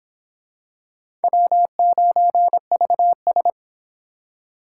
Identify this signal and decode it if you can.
Morse code, heard as a keyed tone, about 24 wpm, 700 Hz. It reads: W9VS